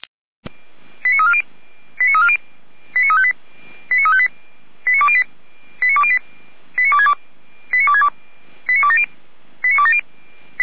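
Five-tone selective call (ZVEI), heard as quick runs of five tones, tones of about 70 ms. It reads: 89340 89358 89358 89208 89208 89262 89262 89270
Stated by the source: ZVEI five-tone selective call received by SDR, fire brigade Feuerwehr Orlamünde.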